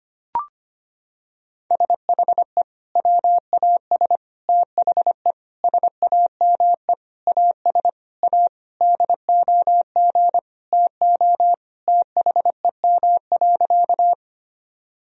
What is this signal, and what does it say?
Morse code, heard as a keyed tone, about 25 wpm, 700 Hz. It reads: SHE WAS THE SAME AS A DOG TO THEM.
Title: Sentences from Top 1000 Words - Review Part 2 - 25wpm